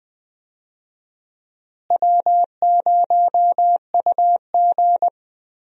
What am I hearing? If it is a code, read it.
Morse code, heard as a keyed tone, about 20 wpm, 700 Hz. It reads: W0UG
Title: USA Call Signs - 4 Characters - Repeated 3x - 20wpm